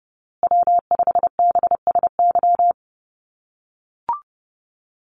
Morse code, heard as a keyed tone, about 30 wpm, 700 Hz. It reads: W5BSY
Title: USA Call Signs Repeated 2x 30wpm